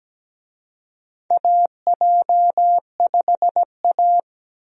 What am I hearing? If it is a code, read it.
Morse code, heard as a keyed tone, about 17 wpm, 700 Hz. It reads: AJ5A
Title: USA Call Signs - 4 Characters - Repeated 2x - 17wpm